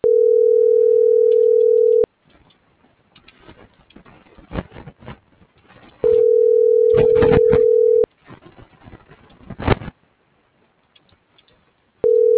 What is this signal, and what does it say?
Ringback tone (North America), 440 + 480 Hz.